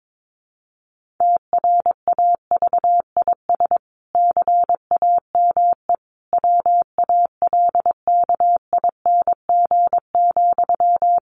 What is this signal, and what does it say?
Morse code, heard as a keyed tone, about 22 wpm, 700 Hz. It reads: TRAVIS CAME WALKING,
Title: A Sound Of Thunder 22wpm